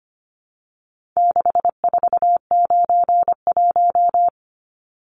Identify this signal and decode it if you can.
Morse code, heard as a keyed tone, about 25 wpm, 700 Hz. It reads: 6491
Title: Numbers - 4-Digit 25wpm